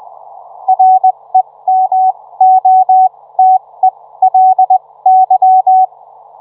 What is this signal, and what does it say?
Morse code, heard as a keyed tone, about 20 wpm, spaced as for 16 wpm, 750 Hz. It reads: REMOTELY